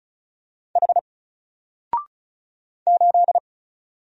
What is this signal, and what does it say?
Morse code, heard as a keyed tone, about 33 wpm, 700 Hz. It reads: H 8
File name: Single-Letter-Number-35wpm